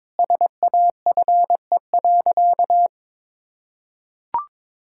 Morse code, heard as a keyed tone, about 22 wpm, 700 Hz. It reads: SAFE.
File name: practice / 22wpm / Sentences with 2-Word Phrases - 22wpm